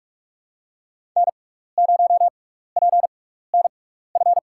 Morse code, heard as a keyed tone, about 46 wpm, spaced as for 15 wpm, 690 Hz. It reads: N0PNF
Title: USA Call Signs - 5 Characters - 6x Character Spacing - 45wpm